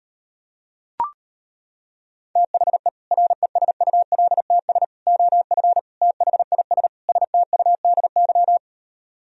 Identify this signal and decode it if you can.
Morse code, heard as a keyed tone, about 33 wpm, 700 Hz. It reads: THE RESULTS OF THIS STUDY